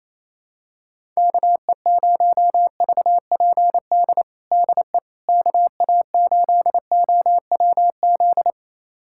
Morse code, heard as a keyed tone, about 28 wpm, 700 Hz. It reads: KE0VPD DE KA8OWZ
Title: USA Callsign DE Callsign - 6 Characters - 28wpm